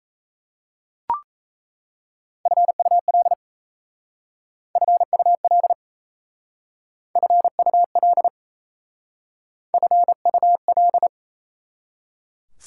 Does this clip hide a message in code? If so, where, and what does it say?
Yes: Morse code from 2.5 to 11.1 s, heard as a keyed tone, about 37 wpm, 700 Hz. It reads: FUL FUL FUL FUL